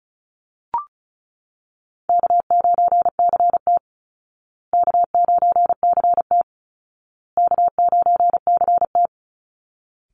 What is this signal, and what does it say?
Morse code, heard as a keyed tone, about 34 wpm, 700 Hz. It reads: K9CT K9CT K9CT
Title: Easy Call Signs Repeated 3x - 35wpm